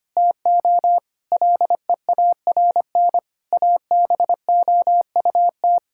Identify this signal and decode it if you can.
Morse code, heard as a keyed tone, about 25 wpm, 700 Hz. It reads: TO LEARN ABOUT
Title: Sentences from Top 1000 Words - Review Part 6 - 25wpm